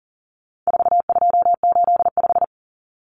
Morse code, heard as a keyed tone, about 40 wpm, 700 Hz. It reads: 4285
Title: Common Cut Numbers - 4-Digit 40wpm